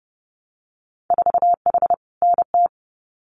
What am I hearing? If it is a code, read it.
Morse code, heard as a keyed tone, about 30 wpm, 700 Hz. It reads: 4H NT
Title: Winter Field Day Exchanges Encore 30wpm